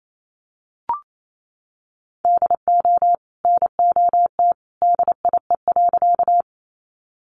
Morse code, heard as a keyed tone, about 28 wpm, 700 Hz. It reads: DO NOT DIE.
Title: Sentences from Top 1300 Words 28wpm